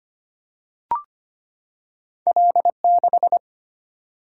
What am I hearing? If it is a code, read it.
Morse code, heard as a keyed tone, about 25 wpm, 690 Hz. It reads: L6
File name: Two Letter-Number Combinations - 25wpm